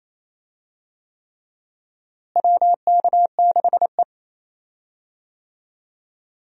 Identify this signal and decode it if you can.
Morse code, heard as a keyed tone, about 28 wpm, 700 Hz. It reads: WK6E
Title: USA Call Signs - 4 Characters - Repeated 3x - 28wpm